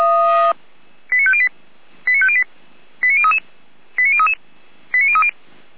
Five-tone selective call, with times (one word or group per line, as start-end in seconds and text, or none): 1.1-1.5 89508
2.1-2.4 89508
3.0-3.4 89033
4.0-4.3 89033
4.9-5.3 89030